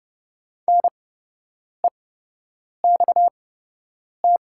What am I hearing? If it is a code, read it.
Morse code, heard as a keyed tone, about 30 wpm, spaced as for 8 wpm, 700 Hz. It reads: NEXT
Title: 8x Character Spacing - Top 300 Words - 30wpm